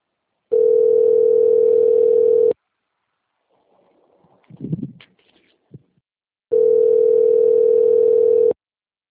Ringback tone (North America), 440 + 480 Hz.